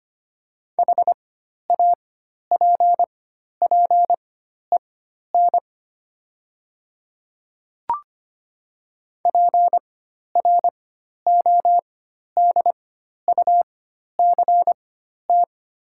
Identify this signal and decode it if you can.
Morse code, heard as a keyed tone, about 25 wpm, spaced as for 12 wpm, 700 Hz. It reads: HAPPEN PRODUCT
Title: 4x Character Spacing - Top 500 Words - 25wpm